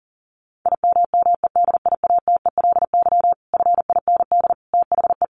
Morse code, heard as a keyed tone, about 40 wpm, 700 Hz. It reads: IMMEDIATELY FIND THE
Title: Sentences from Top 1200 Words 40wpm